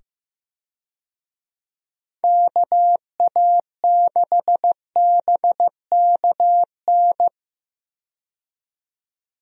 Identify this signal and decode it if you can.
Morse code, heard as a keyed tone, about 15 wpm, 700 Hz. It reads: KA6BKN
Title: USA Call Signs - 6 Characters - 15wpm